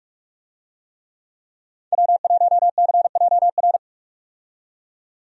Morse code, heard as a keyed tone, about 46 wpm, 700 Hz. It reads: W1CJR
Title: USA Call Signs - 5 Characters - 45wpm